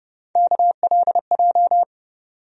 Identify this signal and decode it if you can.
Morse code, heard as a keyed tone, about 31 wpm, 700 Hz. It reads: KLJ
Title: USA Call Sign Postfixes - 3 Letters 30wpm